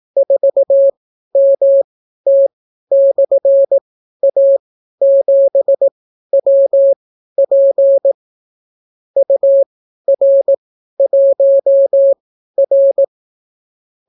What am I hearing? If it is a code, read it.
Morse code, heard as a keyed tone, about 18 wpm, spaced as for 12 wpm, 550 Hz. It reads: 4MT/A7WP UR1R